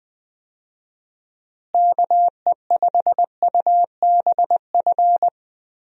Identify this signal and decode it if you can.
Morse code, heard as a keyed tone, about 19 wpm, 700 Hz. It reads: KE5UBF
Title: USA Call Signs - 6 Characters - Repeated 2x - 20wpm